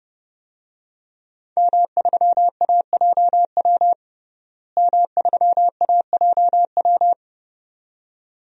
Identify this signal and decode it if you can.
Morse code, heard as a keyed tone, about 32 wpm, 700 Hz. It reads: M3AJW M3AJW
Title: DXCC-100 Call Signs - 5 Characters - Repeated 2x - 30wpm